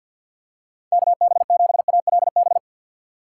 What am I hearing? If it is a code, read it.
Morse code, heard as a keyed tone, about 46 wpm, 700 Hz. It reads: KB7ALB
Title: USA Call Signs - 6 Characters - 6x Character Spacing - 50wpm